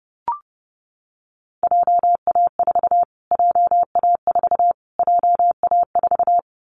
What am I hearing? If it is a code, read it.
Morse code, heard as a keyed tone, about 30 wpm, 700 Hz. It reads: JA4 JA4 JA4
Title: Callsign Prefixes for Top 25 DXCC - Repeated 3x - 30wpm